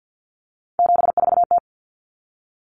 Morse code, heard as a keyed tone, about 50 wpm, 700 Hz. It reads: 74T
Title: Common Cut Numbers - 3-Digit 50wpm